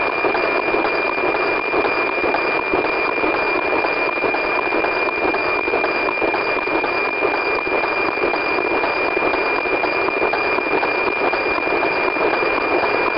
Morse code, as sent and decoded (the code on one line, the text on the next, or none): none